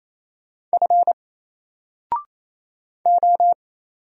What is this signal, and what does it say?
Morse code, heard as a keyed tone, about 27 wpm, 700 Hz. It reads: F O
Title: Single Character (Letter:Number:Punctuation:Prosign) 28wpm